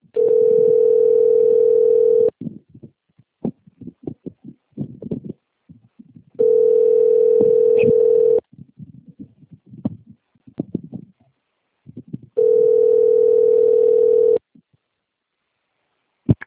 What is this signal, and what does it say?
Ringback tone (North America), 440 + 480 Hz.
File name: rg-1600-unknown- - - .54414